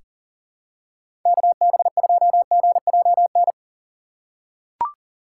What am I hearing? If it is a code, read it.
Morse code, heard as a keyed tone, about 40 wpm, 700 Hz. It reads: KB2GJN
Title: USA Call Signs - 2x Character Spacing - 40wpm